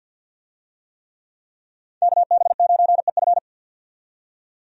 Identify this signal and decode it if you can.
Morse code, heard as a keyed tone, about 50 wpm, 700 Hz. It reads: KB9EF
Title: USA Call Signs - 5 Characters - 50wpm